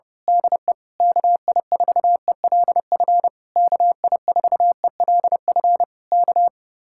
Morse code, heard as a keyed tone, about 27 wpm, 700 Hz. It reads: DE KI4ELF KI4ELF K
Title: Speed-Racing Common QSO Phrases Encore - 25wpm